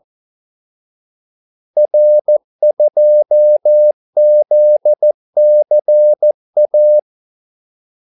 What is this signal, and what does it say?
Morse code, heard as a keyed tone, about 14 wpm, 600 Hz. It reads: R2ZCA
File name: Callsigns-14wpm